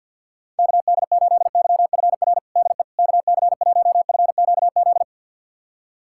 Morse code, heard as a keyed tone, about 52 wpm, 700 Hz. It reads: KD8YFR DE KC1FQZ
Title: USA Callsign DE Callsign 50wpm